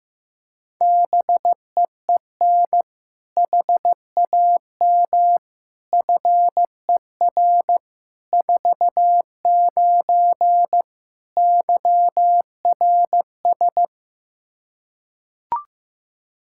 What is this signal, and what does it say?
Morse code, heard as a keyed tone, about 16 wpm, 700 Hz. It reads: BEEN HAM FER 49 YRS
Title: K1USN SST QSO Practice 15wpm